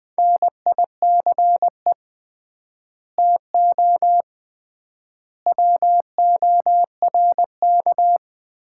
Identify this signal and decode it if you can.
Morse code, heard as a keyed tone, about 20 wpm, 700 Hz. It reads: NICE TO WORK
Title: Common QSO Phrases - 3x Word-Spacing - 20wpm